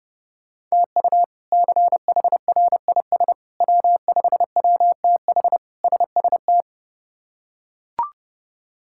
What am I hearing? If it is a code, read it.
Morse code, heard as a keyed tone, about 30 wpm, 700 Hz. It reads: TU CHRIS W5WTH SST
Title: K1USN SST QSO Practice - 30wpm